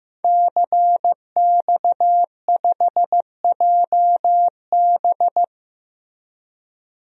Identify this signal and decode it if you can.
Morse code, heard as a keyed tone, about 15 wpm, 700 Hz. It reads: CX5JB
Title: DXCC-100 Call Signs - 5 Characters - Repeated 3x - 15wpm